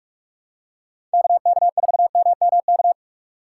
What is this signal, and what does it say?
Morse code, heard as a keyed tone, about 44 wpm, 700 Hz. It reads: KK4MMK
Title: USA Call Signs - 4x Character Spacing - 45wpm